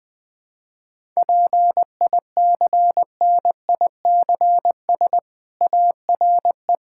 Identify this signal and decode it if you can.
Morse code, heard as a keyed tone, about 20 wpm, 700 Hz. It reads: PICNICS ARE
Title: Difficult Sentences with Hints - 20wpm